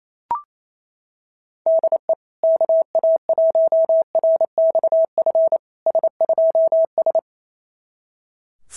Morse code, heard as a keyed tone, about 27 wpm, 660 Hz. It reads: DE KA1RXF S2S
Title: SOTA QSO Practice 28wpm